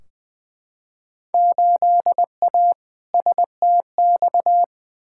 Morse code, heard as a keyed tone, about 20 wpm, 700 Hz. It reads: 8A STX
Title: Field Day Exchanges Encore - 20wpm